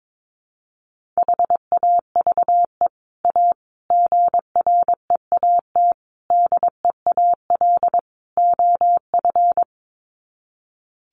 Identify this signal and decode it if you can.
Morse code, heard as a keyed tone, about 22 wpm, 700 Hz. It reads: HAVE A GREAT DEAL OF
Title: Sets of 5 Words Encore2 22wpm